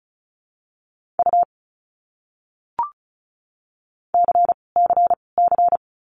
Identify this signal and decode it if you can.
Morse code, heard as a keyed tone, about 34 wpm, 700 Hz. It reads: U C C C
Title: Abbreviations Repeated 3x - 35wpm